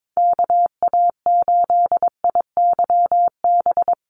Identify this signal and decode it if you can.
Morse code, heard as a keyed tone, about 22 wpm, 700 Hz. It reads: KA8IYB
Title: USA Call Signs 22wpm